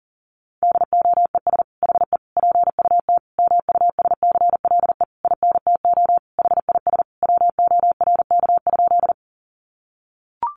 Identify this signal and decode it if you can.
Morse code, heard as a keyed tone, about 40 wpm, 700 Hz. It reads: DOES HE PUT MUSCLE INTO HIS WORK?
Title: Sentences from Top 1300 Words 40wpm